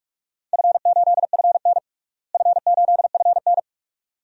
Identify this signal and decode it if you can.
Morse code, heard as a keyed tone, about 45 wpm, 700 Hz. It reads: F8FN F8FN
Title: DXCC-100 Call Signs - 4 Characters - Repeated 2x - 45wpm